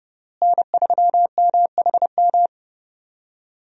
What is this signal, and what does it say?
Morse code, heard as a keyed tone, about 30 wpm, 700 Hz. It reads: N3MHM
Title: USA Call Signs - 5 Characters - Repeated 3x - 30wpm